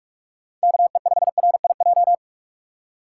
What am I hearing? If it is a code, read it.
Morse code, heard as a keyed tone, about 43 wpm, 700 Hz. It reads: KE5RIJ
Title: USA Call Signs - 4x Character Spacing - 45wpm